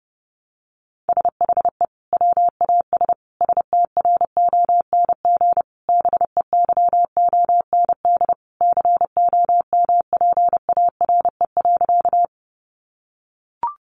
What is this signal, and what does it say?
Morse code, heard as a keyed tone, about 30 wpm, 700 Hz. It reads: SHE WAS STRONG BEYOND COMPARE.
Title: Sentences from Top 1300 Words 30wpm